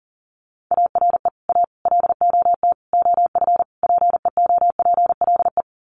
Morse code, heard as a keyed tone, about 40 wpm, 700 Hz. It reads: ARE A LOT OF PEOPLE
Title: Sets of 5 Words 40wpm